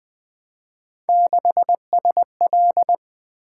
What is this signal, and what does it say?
Morse code, heard as a keyed tone, about 20 wpm, 700 Hz. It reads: 6SL
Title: Three Letter-Number Combinations - 20wpm